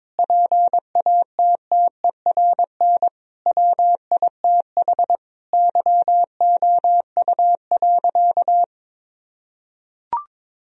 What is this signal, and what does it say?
Morse code, heard as a keyed tone, about 22 wpm, 700 Hz. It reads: PATTERN WITH YOU.
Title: Sentences from Top 1000 Words - Review Part 2 - 22wpm